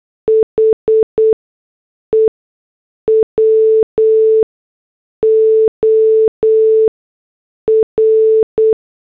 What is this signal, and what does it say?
Morse code, heard as a keyed tone, about 8 wpm, spaced as for 6 wpm, 440 Hz. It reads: HE WOR